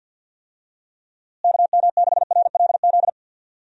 Morse code, heard as a keyed tone, about 47 wpm, 700 Hz. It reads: KM6RLZ
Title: USA Call Signs - 6 Characters - Repeated 3x - 50wpm